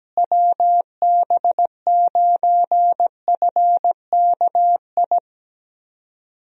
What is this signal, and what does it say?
Morse code, heard as a keyed tone, about 17 wpm, 700 Hz. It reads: WB9FKI
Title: USA Call Signs - 6 Characters - Repeated 3x - 17wpm